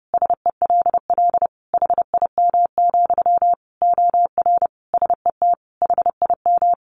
Morse code, heard as a keyed tone, about 30 wpm, 700 Hz. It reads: SELL HIM, OR SET HIM
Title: Sentences from Top 1100 Words 30wpm